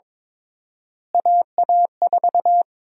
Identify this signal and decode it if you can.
Morse code, heard as a keyed tone, about 22 wpm, 700 Hz. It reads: AA4
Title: USA Call Sign Prefixes - Repeated 2x - 22wpm